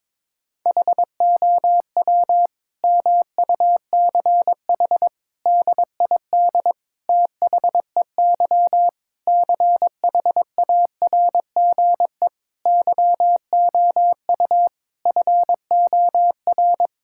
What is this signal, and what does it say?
Morse code, heard as a keyed tone, about 22 wpm, 700 Hz. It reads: HOW MUCH DID THEY CHARGE YOU FOR
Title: Sentences from Top 1000 Words - Review Part 6 - 22wpm